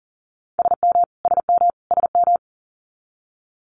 Morse code, heard as a keyed tone, about 40 wpm, 700 Hz. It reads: SM SM SM
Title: Multi-letter Phonemes Repeated 3x 40wpm